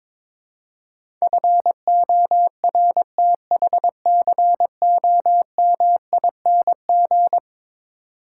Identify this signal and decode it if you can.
Morse code, heard as a keyed tone, about 22 wpm, 700 Hz. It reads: FORTHCOMING
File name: practice / 22wpm / Eleven Letter Words - 22wpm